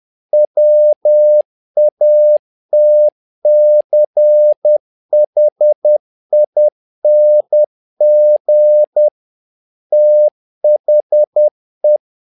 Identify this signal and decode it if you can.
Morse code, heard as a keyed tone, about 10 wpm, 600 Hz. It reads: WATCHING THE
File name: learning center / tower trouble / tower trouble 10wpm